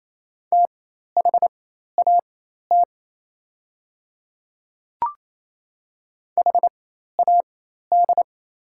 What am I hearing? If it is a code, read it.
Morse code, heard as a keyed tone, about 28 wpm, spaced as for 13 wpm, 700 Hz. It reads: THAT HAD
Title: 4x Character Spacing - Top 100 Words - 28wpm